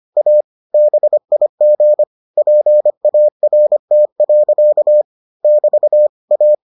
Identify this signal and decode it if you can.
Morse code, heard as a keyed tone, about 25 wpm, 600 Hz. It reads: A BIG PART. <BT> A